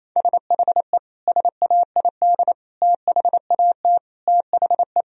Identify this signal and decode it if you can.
Morse code, heard as a keyed tone, about 28 wpm, 700 Hz. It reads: SHE SAID THAT THE
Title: Sentences from Top 1000 Words - Review Part 4 - 28wpm